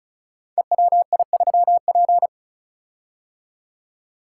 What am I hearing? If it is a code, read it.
Morse code, heard as a keyed tone, about 36 wpm, 700 Hz. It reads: EWI3P
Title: Code Groups - 5 Letter:Numbers - 35wpm